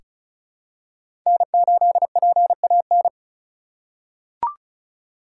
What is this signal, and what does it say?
Morse code, heard as a keyed tone, about 36 wpm, 700 Hz. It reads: N8PAN
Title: USA Call Signs - 3x Character Spacing - 35wpm